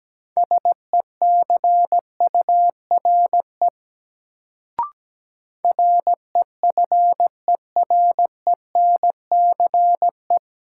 Morse code, heard as a keyed tone, about 17 wpm, 700 Hz. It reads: SECURE REFERENCE
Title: Sentences from Top 1200 Words - New Word Drill Alt - 17wpm